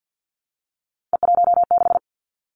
Morse code, heard as a keyed tone, about 50 wpm, 700 Hz. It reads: E16
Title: Three Challenging Letter-Number Combinations 50wpm